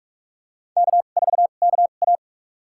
Morse code, heard as a keyed tone, about 46 wpm, spaced as for 33 wpm, 700 Hz. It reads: K4KA
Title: USA Call Signs - 2x Character Spacing - 45wpm